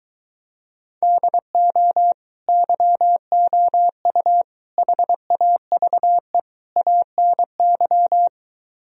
Morse code, heard as a keyed tone, about 23 wpm, 700 Hz. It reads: DO YOU HAVE ANY